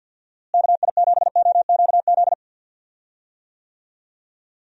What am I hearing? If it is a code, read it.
Morse code, heard as a keyed tone, about 48 wpm, 700 Hz. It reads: KI7OQZ
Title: USA Call Signs - 6 Characters - Repeated 3x - 50wpm